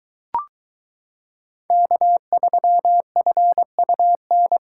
Morse code, heard as a keyed tone, about 22 wpm, 700 Hz. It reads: K3FUN